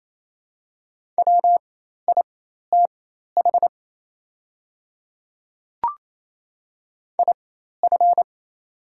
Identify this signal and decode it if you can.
Morse code, heard as a keyed tone, about 28 wpm, spaced as for 13 wpm, 700 Hz. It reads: WITH IF
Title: 4x Character Spacing - Top 300 Words - 28wpm